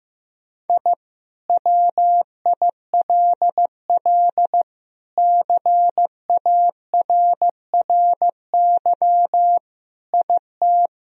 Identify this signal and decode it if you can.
Morse code, heard as a keyed tone, about 15 wpm, 700 Hz. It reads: I WILL CARRY IT
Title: Sentences from Top 1000 Words - Review Part 1 - 15wpm